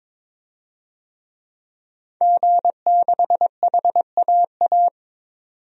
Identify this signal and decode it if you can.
Morse code, heard as a keyed tone, about 22 wpm, 700 Hz. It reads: G6HAA
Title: DXCC-100 Call Signs - 5 Characters - 22wpm